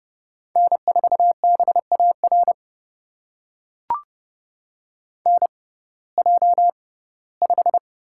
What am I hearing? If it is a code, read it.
Morse code, heard as a keyed tone, about 30 wpm, spaced as for 18 wpm, 700 Hz. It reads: N4BAR NJ5
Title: USA Call Signs - 6x Character Spacing - 30wpm